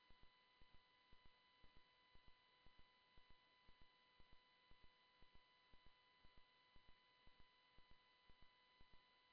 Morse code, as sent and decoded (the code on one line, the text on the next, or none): none